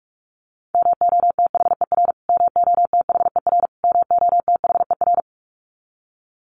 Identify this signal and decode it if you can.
Morse code, heard as a keyed tone, about 44 wpm, 700 Hz. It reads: MOTHER MOTHER MOTHER
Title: Top 200 Words Repeated 45wpm